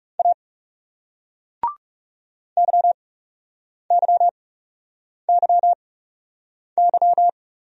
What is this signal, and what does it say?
Morse code, heard as a keyed tone, about 38 wpm, spaced as for 8 wpm, 700 Hz. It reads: A YYYY